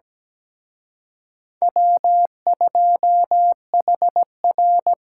Morse code, heard as a keyed tone, about 20 wpm, 700 Hz. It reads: W2HR
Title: Speed Racing - USA 1x2 Call Signs 17wpm